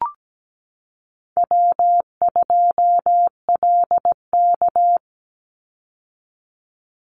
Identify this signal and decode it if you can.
Morse code, heard as a keyed tone, about 17 wpm, 700 Hz. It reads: W2LK
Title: Easy Call Signs - 17wpm